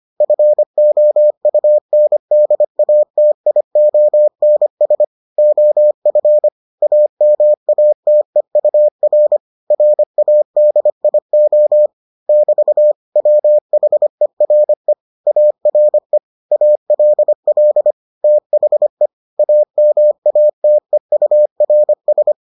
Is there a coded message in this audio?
Yes: Morse code from 0.2 to 22.3 s, heard as a keyed tone, about 25 wpm, 600 Hz. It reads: FOUNDATIONS OF AMATEUR RADIO <BT> WHERE ARE ALL THE AMATEURS